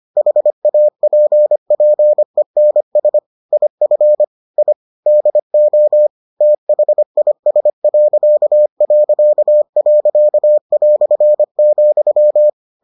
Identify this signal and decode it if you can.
Morse code, heard as a keyed tone, about 25 wpm, 600 Hz. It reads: HAPPENS IF I DO THIS...",